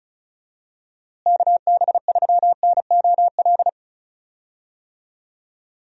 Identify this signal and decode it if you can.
Morse code, heard as a keyed tone, about 34 wpm, 700 Hz. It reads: KB3NOL
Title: USA Call Signs - 6 Characters - Repeated 2x - 35wpm